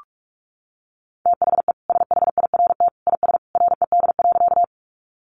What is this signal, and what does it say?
Morse code, heard as a keyed tone, about 44 wpm, 700 Hz. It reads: THE SHIRT IS RED.
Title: Additional Easy Phrases 45wpm